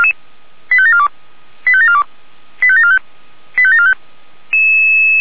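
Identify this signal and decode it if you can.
Five-tone selective call (ZVEI), heard as quick runs of five tones, tones of about 70 ms. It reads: 85742 85742 85745 85745
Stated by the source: ZVEI five-tone call received on SDR, feuerwehr-orlamuende.org (fire brigade Orlamünde).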